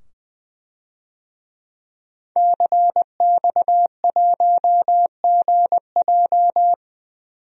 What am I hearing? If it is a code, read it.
Morse code, heard as a keyed tone, about 20 wpm, 700 Hz. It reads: CX1GJ
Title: DXCC-100 Call Signs - 5 Characters - 20wpm